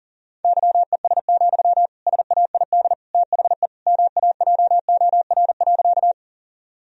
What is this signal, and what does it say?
Morse code, heard as a keyed tone, about 40 wpm, 700 Hz. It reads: YES, SAID THE MAJOR.
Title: Sentences from Top 1000 Words - Review Part 4 - 40wpm